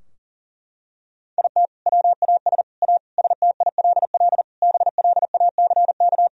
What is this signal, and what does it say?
Morse code, heard as a keyed tone, about 40 wpm, 700 Hz. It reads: IT WAS A STILL BLACK
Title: Sentences from Top 1000 Words - Review Part 2 - 40wpm